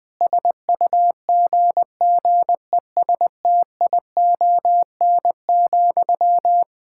Morse code, heard as a keyed tone, about 21 wpm, 700 Hz. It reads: SUGGESTION,
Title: Sentences from Top 1600 Words 20wpm